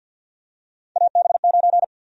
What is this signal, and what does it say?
Morse code, heard as a keyed tone, about 52 wpm, 700 Hz. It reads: AB9
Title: USA Call Sign Prefixes - Repeated 2x - 50wpm